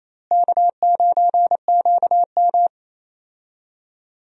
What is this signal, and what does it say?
Morse code, heard as a keyed tone, about 27 wpm, 700 Hz. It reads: K9QM
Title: USA Call Signs - 4 Characters - Repeated 2x - 28wpm